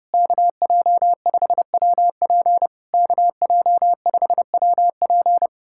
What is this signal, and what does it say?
Morse code, heard as a keyed tone, about 30 wpm, 700 Hz. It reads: KJ5WP KJ5WP
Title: SOTA QSO Practice 30wpm